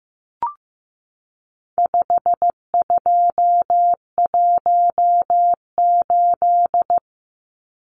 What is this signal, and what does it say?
Morse code, heard as a keyed tone, about 15 wpm, 700 Hz. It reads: 5218
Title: Numbers - 4-Digit 15wpm